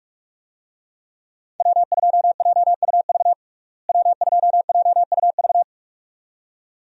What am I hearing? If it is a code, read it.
Morse code, heard as a keyed tone, about 46 wpm, 700 Hz. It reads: W2JUV W2JUV
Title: USA Call Signs - 5 Characters - Repeated 2x - 45wpm